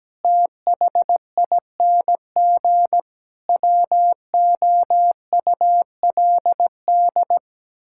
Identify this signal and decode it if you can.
Morse code, heard as a keyed tone, about 17 wpm, 700 Hz. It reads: THING WOULD
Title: Sentences from Top 1000 Words - Review Part 6 - 17wpm